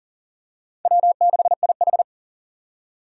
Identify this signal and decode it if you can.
Morse code, heard as a keyed tone, about 40 wpm, 700 Hz. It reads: W6IH
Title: USA Call Signs - 4 Characters - 40wpm